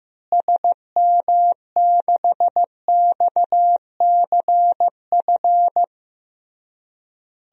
Morse code, heard as a keyed tone, about 15 wpm, 700 Hz. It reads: SM6XCF
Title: DXCC-100 Call Signs - 6 Characters - Repeated 3x - 15wpm